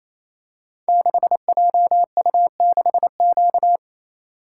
Morse code, heard as a keyed tone, about 28 wpm, 700 Hz. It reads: 6JU6Q
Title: Code Groups - 5 Letter:Numbers - 28wpm